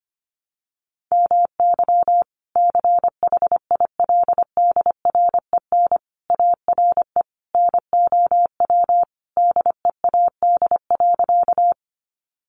Morse code, heard as a keyed tone, about 25 wpm, 700 Hz. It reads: MY CHILDREN ARE NOW DEAD.
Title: Sentences from Top 1500 Words 25wpm